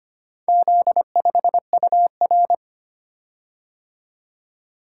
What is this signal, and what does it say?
Morse code, heard as a keyed tone, about 25 wpm, 700 Hz. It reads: Z5UR
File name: practice / 25wpm / Four Letter-Number Combinations - 25wpm